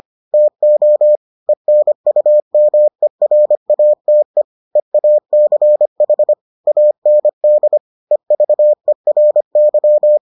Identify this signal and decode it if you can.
Morse code, heard as a keyed tone, about 25 wpm, 600 Hz. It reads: TO ENUMERATE EACH AND EVERY